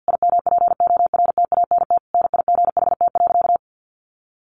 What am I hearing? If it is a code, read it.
Morse code, heard as a keyed tone, about 50 wpm, 700 Hz. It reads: IMPORTANT NIGHT.
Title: Sentences from Top 1300 Words 50wpm